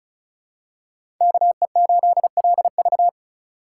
Morse code, heard as a keyed tone, about 36 wpm, 700 Hz. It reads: KE8LV
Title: USA Call Signs - 5 Characters - 4x Character Spacing - 35wpm